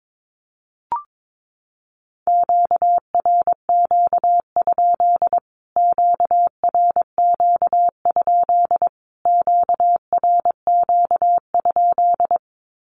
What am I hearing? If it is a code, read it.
Morse code, heard as a keyed tone, about 22 wpm, 700 Hz. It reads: QRQ? QRQ? QRQ?
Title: Q-Codes Repeated 3x 22wpm